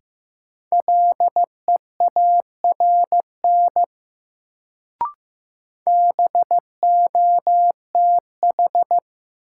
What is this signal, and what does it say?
Morse code, heard as a keyed tone, about 15 wpm, 700 Hz. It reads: LEARN BOTH
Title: Sentences from Top 300 Words - New Word Drill - 15wpm